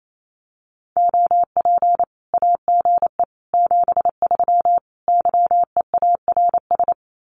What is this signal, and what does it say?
Morse code, heard as a keyed tone, about 28 wpm, 700 Hz. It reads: OP AGE 73 YEARS
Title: Common QSO Phrases 28wpm